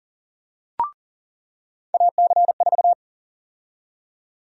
Morse code, heard as a keyed tone, about 40 wpm, 700 Hz. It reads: AC4
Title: USA Call Sign Prefixes - 40wpm